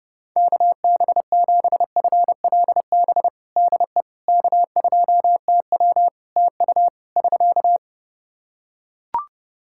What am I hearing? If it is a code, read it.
Morse code, heard as a keyed tone, about 30 wpm, 700 Hz. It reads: KB7FLB DE K2TW TU <SK>
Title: Practice QSO - Exchange 4 - Ending - 30wpm